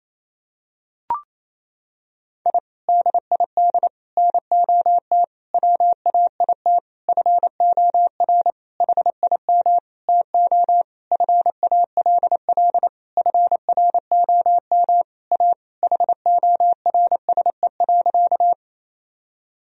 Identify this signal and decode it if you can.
Morse code, heard as a keyed tone, about 28 wpm, 700 Hz. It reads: I DID NOT WAIT FOR HIM TO FALL FROM A HORSE.
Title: Sentences from Top 1000 Words - Review Part 2 - 28wpm